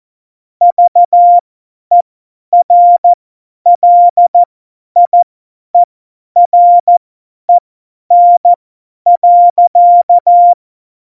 Morse code, heard as a keyed tone, about 14 wpm, spaced as for 10 wpm, 700 Hz. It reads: VERLIEREN.
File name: Morsekurs / Witze / Witz 11 August 2020, 50 F70